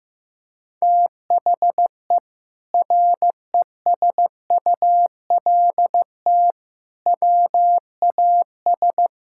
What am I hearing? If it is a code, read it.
Morse code, heard as a keyed tone, about 15 wpm, 700 Hz. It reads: THE RESULT WAS
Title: Sentences from Top 1000 Words - Review Part 5 - 15wpm